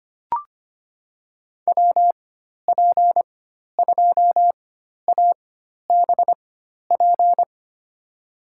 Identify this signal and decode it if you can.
Morse code, heard as a keyed tone, about 25 wpm, spaced as for 12 wpm, 700 Hz. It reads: WP2ABP
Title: USA Call Signs - 6 Characters - 4x Character Spacing - 25wpm